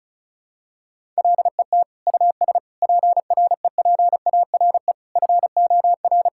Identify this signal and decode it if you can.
Morse code, heard as a keyed tone, about 36 wpm, 700 Hz. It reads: LET US PREPARE FOR
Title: Sentences from Top 1000 Words - Review Part 6 - 35wpm